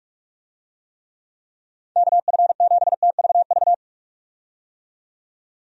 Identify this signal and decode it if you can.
Morse code, heard as a keyed tone, about 46 wpm, 700 Hz. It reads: KF7TVV
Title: USA Call Signs - 6 Characters - 45wpm